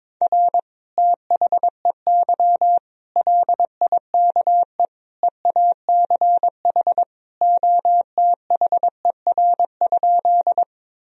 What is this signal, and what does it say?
Morse code, heard as a keyed tone, about 22 wpm, 700 Hz. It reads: R THEY LIKE EACH OTHER?
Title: Sentences from Top 200 Words and Abbreviations 22wpm